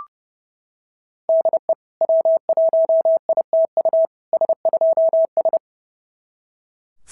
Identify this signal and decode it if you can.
Morse code, heard as a keyed tone, about 30 wpm, 650 Hz. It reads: DE W1ITU S2S
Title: SOTA QSO Practice 30wpm